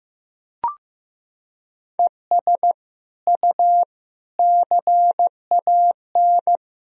Morse code, heard as a keyed tone, about 15 wpm, 700 Hz. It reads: ES U CAN
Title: Sentences from Top 400 Words and Abbreviations 15wpm